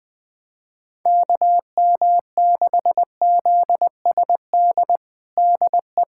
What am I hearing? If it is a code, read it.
Morse code, heard as a keyed tone, about 20 wpm, 700 Hz. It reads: KM6ZSD DE
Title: USA Callsign DE Callsign - 6 Characters - 20wpm